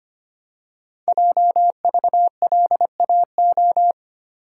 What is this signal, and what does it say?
Morse code, heard as a keyed tone, about 25 wpm, 700 Hz. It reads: JVLAO
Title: Code Groups - 5 Letter:Numbers - 25wpm